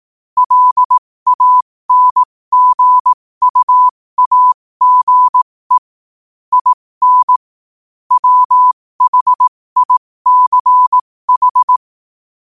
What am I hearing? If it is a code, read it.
Morse code, heard as a keyed tone, about 18 wpm, spaced as for 15 wpm, 1000 Hz. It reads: LANGUAGE IN WHICH
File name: qotdfiles / 15wpm